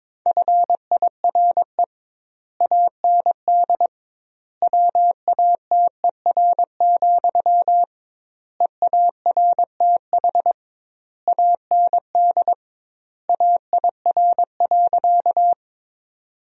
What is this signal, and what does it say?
Morse code, heard as a keyed tone, about 22 wpm, 700 Hz. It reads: FIRE AND WATER, EARTH AND AIR.
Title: 2x Word-Spacing - Sentences from Top 400 Words - 22wpm